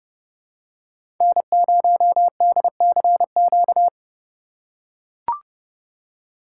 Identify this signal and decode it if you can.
Morse code, heard as a keyed tone, about 30 wpm, 700 Hz. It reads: N0DCQ
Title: USA Call Signs - 5 Characters - 4x Character Spacing - 30wpm